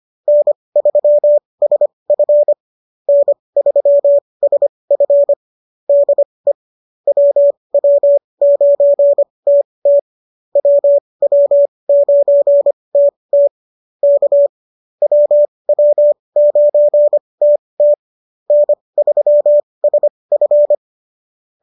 Morse code, heard as a keyed tone, about 25 wpm, spaced as for 20 wpm, 590 Hz. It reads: N3SF N3SF DE WW9TT WW9TT K WW9TT N3SF